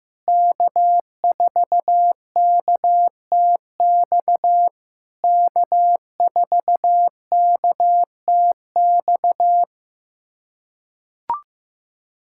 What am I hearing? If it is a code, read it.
Morse code, heard as a keyed tone, about 15 wpm, 700 Hz. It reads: K4KTX K4KTX
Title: SOTA QSO Practice 15wpm